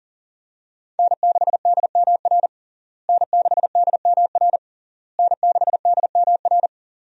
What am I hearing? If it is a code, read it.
Morse code, heard as a keyed tone, about 40 wpm, 700 Hz. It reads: N6DMR N6DMR N6DMR
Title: USA Call Signs - 5 Characters - Repeated 3x - 40wpm